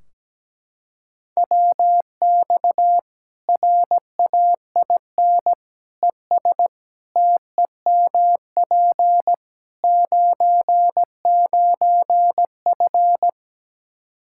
Morse code, heard as a keyed tone, about 17 wpm, 700 Hz. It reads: WX RAIN ES TEMP 99F
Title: Practice QSO - Exchange 2 - Info - 17wpm